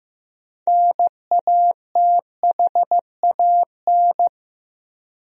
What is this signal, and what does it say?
Morse code, heard as a keyed tone, about 15 wpm, 700 Hz. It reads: NATHAN
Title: USA Common Names - 15wpm